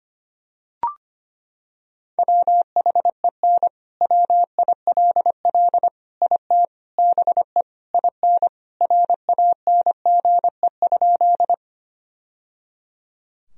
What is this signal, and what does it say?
Morse code, heard as a keyed tone, about 25 wpm, 700 Hz. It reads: WHEN WILL IT BE IN RANGE?
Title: Sentences from Top 1000 Words - Review Part 4 - 25wpm